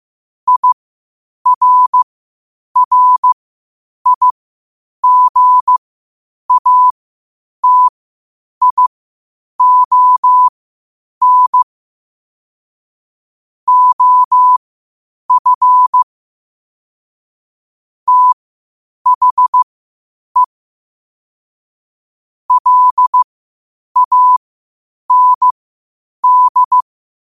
Morse code, heard as a keyed tone, about 15 wpm, spaced as for 8 wpm, 1000 Hz. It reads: IRRIGATION OF THE LAND
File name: qotdfiles / 8wpm